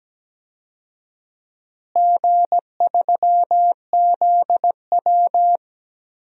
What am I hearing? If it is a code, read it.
Morse code, heard as a keyed tone, about 17 wpm, 700 Hz. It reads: G3ZW
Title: DXCC-100 Call Signs - 4 Characters - Repeated 3x - 17wpm